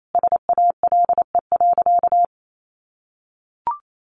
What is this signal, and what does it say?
Morse code, heard as a keyed tone, about 28 wpm, 700 Hz. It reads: SALE.
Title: Sentences from Top 1400 Words 28wpm